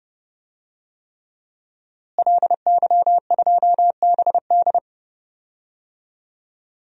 Morse code, heard as a keyed tone, about 32 wpm, 700 Hz. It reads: LY2BD
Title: DXCC-100 Call Signs - 5 Characters - Repeated 3x - 30wpm